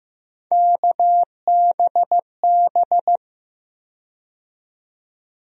Morse code, heard as a keyed tone, about 15 wpm, 700 Hz. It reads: KBB
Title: USA Call Sign Postfixes - 3 Letters 15wpm